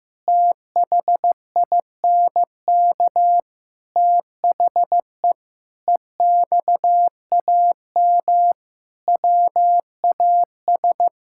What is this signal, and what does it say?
Morse code, heard as a keyed tone, about 15 wpm, 700 Hz. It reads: THINK THE EXAM WAS
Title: Sentences from Top 1000 Words - Review Part 5 - 15wpm